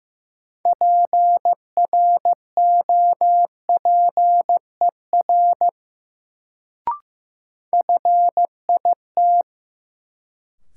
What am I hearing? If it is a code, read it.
Morse code, heard as a keyed tone, about 15 wpm, 700 Hz. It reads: PROPER FIT
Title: Sentences from Top 800 Words - New Word Drill - 15wpm